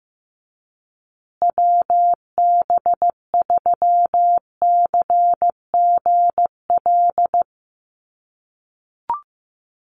Morse code, heard as a keyed tone, about 15 wpm, 700 Hz. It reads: WB3CGL
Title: USA Call Signs Repeated 2x 15wpm